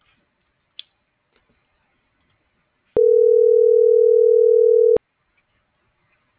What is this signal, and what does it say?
Ringback tone (North America), 440 + 480 Hz.